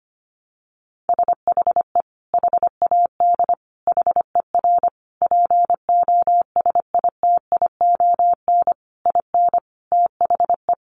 Morse code, heard as a keyed tone, about 25 wpm, 700 Hz. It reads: SHE HAD HER POSITION IN THE
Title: Sentences from Top 900 Words 25wpm